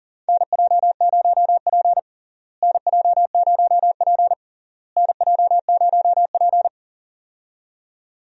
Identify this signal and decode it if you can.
Morse code, heard as a keyed tone, about 38 wpm, 700 Hz. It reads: NJ0P NJ0P NJ0P
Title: USA Call Signs - 4 Characters - Repeated 3x - 40wpm